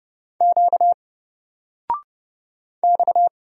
Morse code, heard as a keyed tone, about 30 wpm, 700 Hz. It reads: Q X
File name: Single-Letter-Number-30wpm